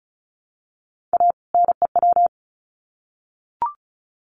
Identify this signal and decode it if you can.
Morse code, heard as a keyed tone, about 34 wpm, 700 Hz. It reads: A NEW
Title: Sets of 2 Words 35wpm